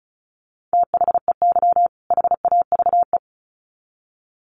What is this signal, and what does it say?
Morse code, heard as a keyed tone, about 34 wpm, 700 Hz. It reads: THEY HAVE
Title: Sets of 2 Words 35wpm